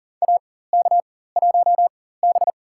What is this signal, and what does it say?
Morse code, heard as a keyed tone, about 40 wpm, spaced as for 19 wpm, 700 Hz. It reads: AK1B
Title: USA Call Signs - 4 Characters - 4x Character Spacing - 40wpm